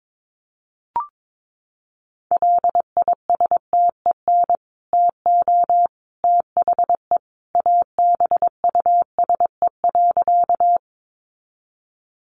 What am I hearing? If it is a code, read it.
Morse code, heard as a keyed tone, about 22 wpm, 700 Hz. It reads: LISTEN TO THE ABUSE.
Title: Sentences from Top 1300 Words 22wpm